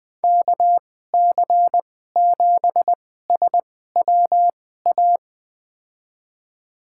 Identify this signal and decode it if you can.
Morse code, heard as a keyed tone, about 20 wpm, spaced as for 14 wpm, 700 Hz. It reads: KC7SWA
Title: USA Call Signs - 6 Characters - 2x Character Spacing - 20wpm